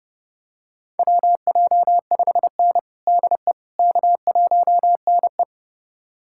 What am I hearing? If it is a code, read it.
Morse code, heard as a keyed tone, about 30 wpm, 700 Hz. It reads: WJ5N DE K1NE
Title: USA Callsign DE Callsign - 4 Characters - 30wpm